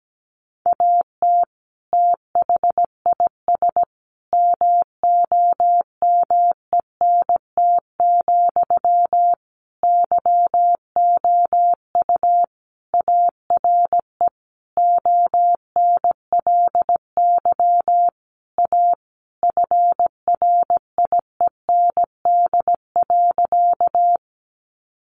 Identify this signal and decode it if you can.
Morse code, heard as a keyed tone, about 17 wpm, 700 Hz. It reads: AT THIS MOMENT, YOU ARE ONLY A FRIEND.
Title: Sentences from Top 1000 Words 17wpm